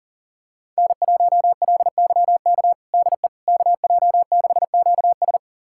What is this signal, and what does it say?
Morse code, heard as a keyed tone, about 40 wpm, 700 Hz. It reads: N1LYK DE KJ6QS
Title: USA Callsign DE Callsign - 5 Characters - 40wpm